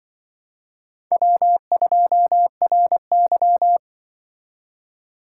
Morse code, heard as a keyed tone, about 25 wpm, 700 Hz. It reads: W2RY
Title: Speed Racing - USA 1x2 Call Signs 20wpm